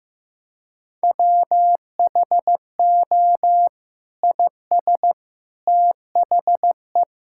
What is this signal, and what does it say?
Morse code, heard as a keyed tone, about 15 wpm, 700 Hz. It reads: WHO IS THE
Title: Sentences from Top 1600 Words 15wpm